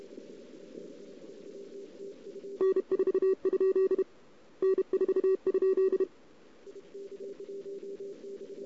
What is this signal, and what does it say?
Morse code, heard as a keyed tone, about 31 wpm, 1140 Hz, with noise. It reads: N4? N4?
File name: CW / 2008-ss-cw-k5tr-op-n5rz -16:00